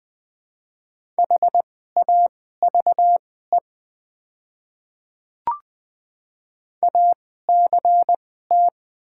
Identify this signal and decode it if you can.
Morse code, heard as a keyed tone, about 20 wpm, spaced as for 14 wpm, 700 Hz. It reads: HAVE ACT
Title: 2x Character Spacing - Top 300 Words - 20wpm